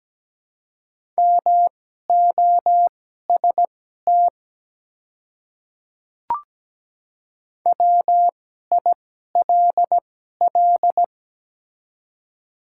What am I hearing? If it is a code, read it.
Morse code, heard as a keyed tone, about 17 wpm, spaced as for 12 wpm, 690 Hz. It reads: MOST WILL
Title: 2x Character Spacing - Top 100 Words - 17wpm